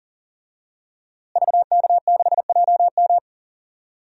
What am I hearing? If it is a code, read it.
Morse code, heard as a keyed tone, about 40 wpm, 700 Hz. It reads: VK6JM
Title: DXCC-100 Call Signs - 5 Characters - Repeated 2x - 40wpm